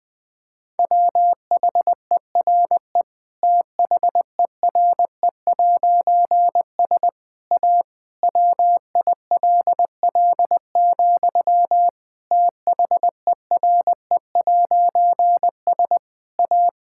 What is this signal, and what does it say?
Morse code, heard as a keyed tone, about 20 wpm, 700 Hz. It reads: WHERE THERE'S A WILL, THERE'S A
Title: Difficult Sentences with Hints - 20wpm